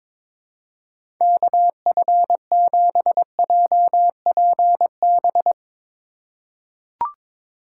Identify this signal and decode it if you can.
Morse code, heard as a keyed tone, about 22 wpm, 700 Hz. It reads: KF7JPB
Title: USA Call Signs - 6 Characters - 2x Character Spacing - 22wpm